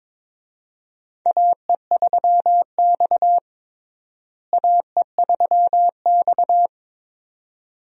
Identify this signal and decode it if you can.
Morse code, heard as a keyed tone, about 22 wpm, 700 Hz. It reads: AE3X AE3X
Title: USA Call Signs - 4 Characters - Repeated 2x - 22wpm